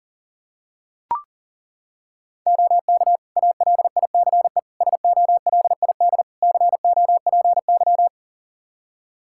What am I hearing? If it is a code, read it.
Morse code, heard as a keyed tone, about 40 wpm, 700 Hz. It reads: OK ALICE SOLID COPY
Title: Practice QSO - Exchange 3 - More Info - 40wpm